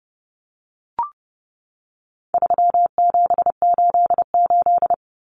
Morse code, heard as a keyed tone, about 30 wpm, 700 Hz. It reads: 3788
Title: Common Cut Numbers - 4-Digit 30wpm